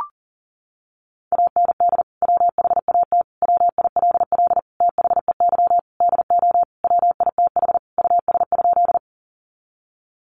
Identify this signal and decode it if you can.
Morse code, heard as a keyed tone, about 40 wpm, 700 Hz. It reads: AND WHAT WILL THEY DO WITH US?
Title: Sentences from Top 200 Words 40wpm